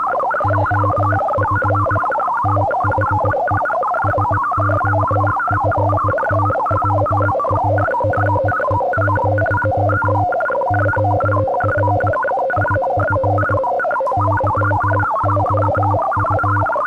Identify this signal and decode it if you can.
Morse code, heard as a keyed tone, about 18 wpm, 310 Hz. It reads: OF THE SOFTWARE, OR IF YOU